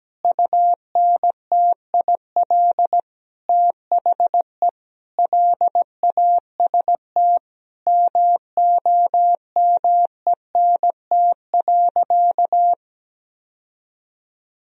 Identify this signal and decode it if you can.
Morse code, heard as a keyed tone, about 17 wpm, 700 Hz. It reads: UNTIL THE LAST MOMENT.
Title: Sentences from Top 1000 Words - Review Part 1 - 17wpm